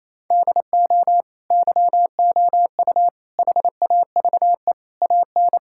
Morse code, heard as a keyed tone, about 28 wpm, 700 Hz. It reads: DO YOU HAVE AN
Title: Sentences from Top 1000 Words - Review Part 5 - 28wpm